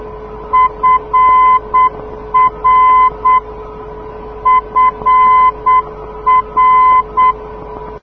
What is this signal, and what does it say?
Morse code, heard as a keyed tone, about 8 wpm, 1020 Hz. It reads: FR FR
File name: europe / d-6028 - ndb frankfurt fr-297 (2004) (rainer brannolte, d) xc